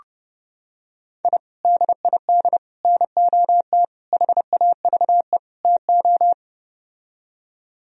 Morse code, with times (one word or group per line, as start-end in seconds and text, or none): 1.2-1.4 I
1.6-2.6 DID
2.8-3.8 NOT
4.1-5.4 HAVE
5.6-6.3 TO